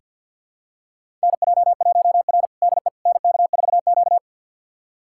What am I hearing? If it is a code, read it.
Morse code, heard as a keyed tone, about 48 wpm, 700 Hz. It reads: NJ1R DE NK4Q
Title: USA Callsign DE Callsign - 4 Characters - 50wpm